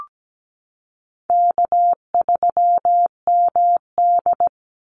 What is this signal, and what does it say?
Morse code, heard as a keyed tone, about 17 wpm, 700 Hz. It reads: K3MD
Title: Easy Call Signs Repeated 3x - 17wpm